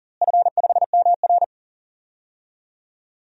Morse code, heard as a keyed tone, about 40 wpm, 700 Hz. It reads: F5MR
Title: DXCC-100 Call Signs - 4 Characters - Repeated 3x - 40wpm